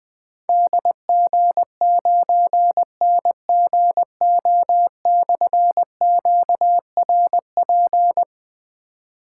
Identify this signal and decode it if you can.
Morse code, heard as a keyed tone, about 20 wpm, 700 Hz. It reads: DG9NGO/QRP
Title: Difficult Call Signs 20wpm